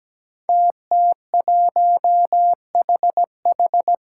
Morse code, heard as a keyed tone, about 17 wpm, 700 Hz. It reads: TT1HH
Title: Five Challenging Letter-Number Combinations 17wpm